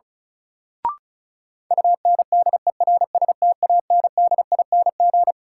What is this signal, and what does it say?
Morse code, heard as a keyed tone, about 36 wpm, 700 Hz. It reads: UNDERSTANDING
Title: Sentences from Top 1000 Words - Review Part 3 - New Word Drill - 35wpm